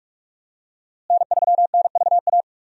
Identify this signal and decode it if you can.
Morse code, heard as a keyed tone, about 44 wpm, 700 Hz. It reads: N3NVA
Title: USA Call Signs - 5 Characters - Repeated 3x - 45wpm